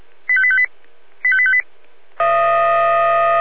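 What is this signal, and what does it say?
Five-tone selective call (ZVEI), heard as quick runs of five tones, tones of about 70 ms. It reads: 85858 85858
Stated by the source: ZVEI five-tone call 85858 received on SDR.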